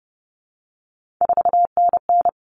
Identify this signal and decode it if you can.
Morse code, heard as a keyed tone, about 30 wpm, 700 Hz. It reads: 4NN
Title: Common Cut Numbers - 3-Digit 30wpm